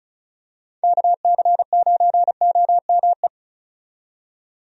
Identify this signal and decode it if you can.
Morse code, heard as a keyed tone, about 36 wpm, 700 Hz. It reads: KC9OME
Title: USA Call Signs - 6 Characters - 35wpm